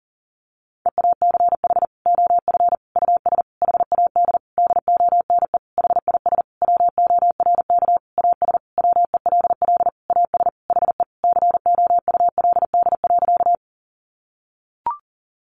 Morse code, heard as a keyed tone, about 40 wpm, 700 Hz. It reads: EACH OF US HAD DONE HIS WORK AS WELL AS HE COULD.
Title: Sentences from Top 500 Words 40wpm